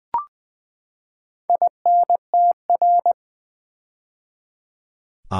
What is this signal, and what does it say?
Morse code, heard as a keyed tone, about 20 wpm, 700 Hz. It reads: INTR
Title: Build A Word - 4 Letters At A Time - 12-Letter Words - 20wpm